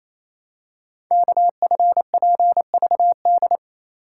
Morse code, heard as a keyed tone, about 28 wpm, 700 Hz. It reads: KFPVD
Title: Code Groups - 5 Letters - 28wpm